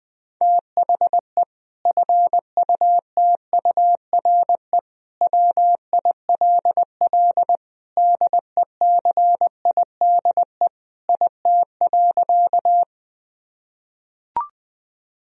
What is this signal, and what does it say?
Morse code, heard as a keyed tone, about 20 wpm, 700 Hz. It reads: THE FUTURE WILL DECIDE IT.
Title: Sentences from Top 1000 Words - Review Part 3 - 20wpm